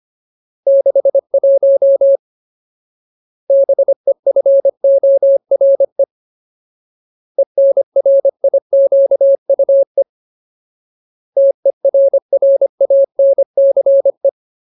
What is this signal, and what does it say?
Morse code, heard as a keyed tone, about 25 wpm, 550 Hz. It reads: 61 BEFORE ENRIQUE TERRANCE
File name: Ultimate Word List / 25WPM Ultimate